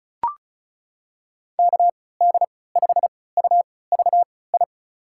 Morse code, heard as a keyed tone, about 36 wpm, spaced as for 20 wpm, 700 Hz. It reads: KD5UVI
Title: USA Call Signs - 6 Characters - 3x Character Spacing - 35wpm